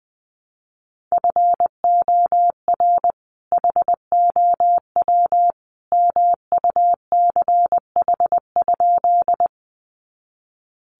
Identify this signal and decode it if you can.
Morse code, heard as a keyed tone, about 20 wpm, 700 Hz. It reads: FOR HOW MUCH?
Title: Sentences from Top 200 Words 20wpm